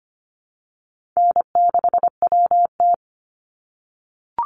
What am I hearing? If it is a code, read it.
Morse code, heard as a keyed tone, about 25 wpm, 700 Hz. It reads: N6WT
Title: NA SOTA Chaser Callsigns 25wpm